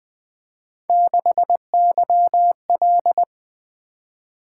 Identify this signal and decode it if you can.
Morse code, heard as a keyed tone, about 21 wpm, 700 Hz. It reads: 6YL
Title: Three Letter-Number Combinations - 20wpm